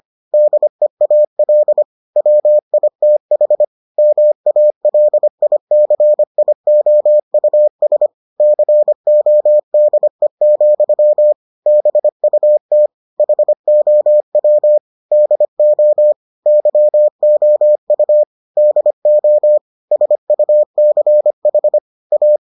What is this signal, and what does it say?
Morse code, heard as a keyed tone, about 25 wpm, 600 Hz. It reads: DEAL WITH MALICIOUS CODE, BUT HOW DO YOU DO SUCH A